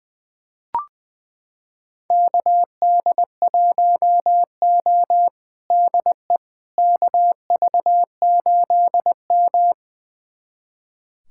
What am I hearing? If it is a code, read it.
Morse code, heard as a keyed tone, about 21 wpm, 700 Hz. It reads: KD1O DE KV8M
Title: USA Callsign DE Callsign - 4 Characters - 20wpm